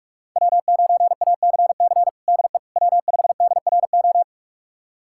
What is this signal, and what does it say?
Morse code, heard as a keyed tone, about 46 wpm, 700 Hz. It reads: W9ACC DE W5DRO
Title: USA Callsign DE Callsign - 5 Characters - 45wpm